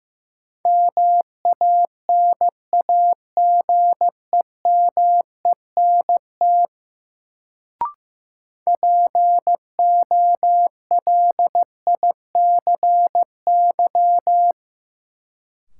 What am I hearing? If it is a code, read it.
Morse code, heard as a keyed tone, about 15 wpm, 700 Hz. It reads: MANAGEMENT POLICY
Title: Sentences from Top 1200 Words - New Word Drill Alt - 15wpm